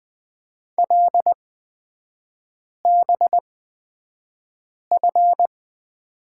Morse code, heard as a keyed tone, about 20 wpm, spaced as for 5 wpm, 700 Hz. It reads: LBF
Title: 236 Lesson 38e - Letter q - Three Characters 20wpm